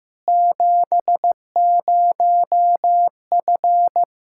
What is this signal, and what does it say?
Morse code, heard as a keyed tone, about 15 wpm, 700 Hz. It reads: 70F
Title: K1USN SST QSO Practice 15wpm